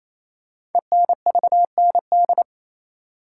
Morse code, heard as a keyed tone, about 27 wpm, 700 Hz. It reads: ENVND
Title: Code Groups - 5 Letters - 28wpm